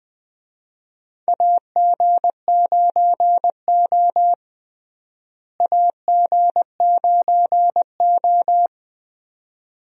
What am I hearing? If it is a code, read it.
Morse code, heard as a keyed tone, about 20 wpm, 700 Hz. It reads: AG9O AG9O
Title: USA Call Signs - 4 Characters - Repeated 2x - 20wpm